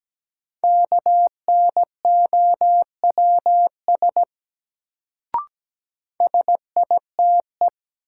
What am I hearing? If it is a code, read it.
Morse code, heard as a keyed tone, about 17 wpm, 700 Hz. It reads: KNOWS SITE
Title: Sentences from Top 1000 Words - Review Part 1 - New Word Drill - 17wpm